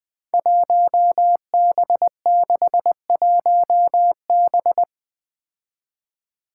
Morse code, heard as a keyed tone, about 20 wpm, 700 Hz. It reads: 1B61B
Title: Five Challenging Letter-Number Combinations 20wpm